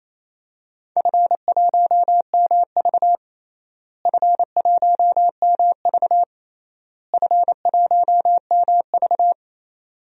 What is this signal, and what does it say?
Morse code, heard as a keyed tone, about 28 wpm, 700 Hz. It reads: F1MV F1MV F1MV
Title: DXCC-100 Call Signs - 4 Characters - Repeated 3x - 28wpm